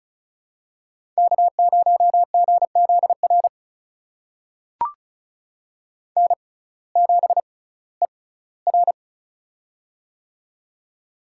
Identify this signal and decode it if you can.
Morse code, heard as a keyed tone, about 36 wpm, spaced as for 19 wpm, 700 Hz. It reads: K0GZR N7ER
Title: USA Call Signs - 6x Character Spacing - 35wpm